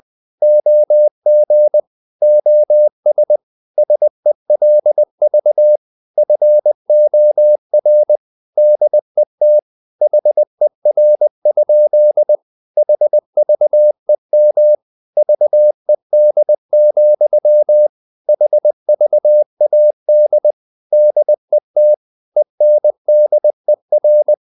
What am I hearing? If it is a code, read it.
Morse code, heard as a keyed tone, about 20 wpm, 600 Hz. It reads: OG OS SELV FOR DET HER? HVEM VED, HVAD DET ENDER
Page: Ferie 20 wpm